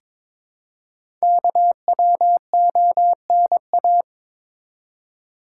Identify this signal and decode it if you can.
Morse code, heard as a keyed tone, about 23 wpm, 700 Hz. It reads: KWONA
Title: Code Groups - 5 Letters - 22wpm